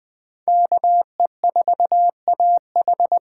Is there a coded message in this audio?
Yes: Morse code from 0.5 to 3.2 s, heard as a keyed tone, about 20 wpm, 700 Hz. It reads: KE4AH